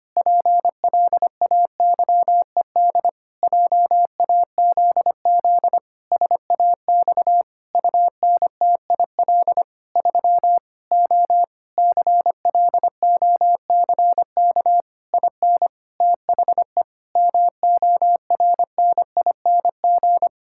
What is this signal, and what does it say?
Morse code, heard as a keyed tone, about 25 wpm, 700 Hz. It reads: PLAYED JAZZ SAX UNTIL 3 O CLOCK IN THE MORNING
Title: Pangram Challenge v2 - 25wpm